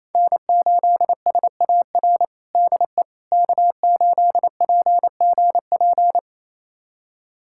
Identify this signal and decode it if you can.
Morse code, heard as a keyed tone, about 28 wpm, 700 Hz. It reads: N8SAR DE K8PGP
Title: USA Callsign DE Callsign - 5 Characters - 28wpm